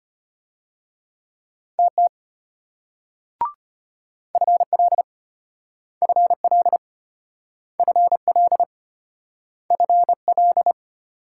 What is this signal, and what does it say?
Morse code, heard as a keyed tone, about 33 wpm, 700 Hz. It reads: TT FL FL FL FL